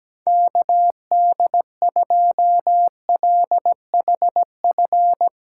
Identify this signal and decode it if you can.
Morse code, heard as a keyed tone, about 17 wpm, 700 Hz. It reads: KD2LHF
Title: USA Callsign DE Callsign - 6 Characters - 17wpm